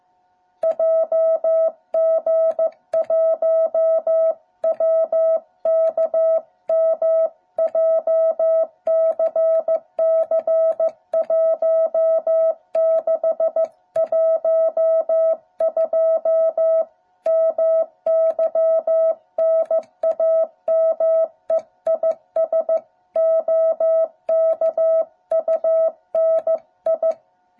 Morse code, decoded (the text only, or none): JG1WKM JCC1612 MY NAME IS OKUNI